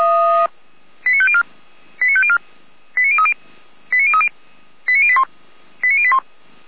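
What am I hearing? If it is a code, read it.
Five-tone selective call (ZVEI), heard as quick runs of five tones, tones of about 70 ms. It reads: 89504 89504 89030 89030 89081 89081
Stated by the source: Five-tone ZVEI selective call, received with an SDR.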